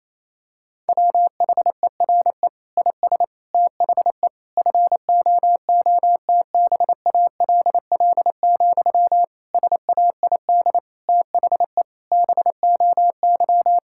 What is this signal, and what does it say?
Morse code, heard as a keyed tone, about 28 wpm, 700 Hz. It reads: WHERE IS THE FOOTBALL, SAID THE BOY
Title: Sentences from Top 1600 Words 28wpm